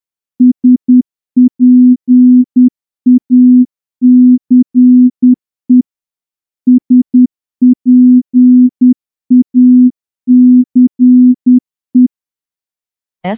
Morse code, heard as a keyed tone, about 10 wpm, 250 Hz. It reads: SPACE SPACE
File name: morse-training-10WPM